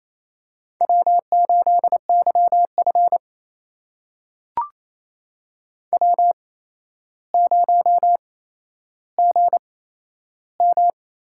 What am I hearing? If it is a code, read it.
Morse code, heard as a keyed tone, about 29 wpm, spaced as for 12 wpm, 700 Hz. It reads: W8YF W0GM
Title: USA Call Signs - 4 Characters - 8x Character Spacing - 28wpm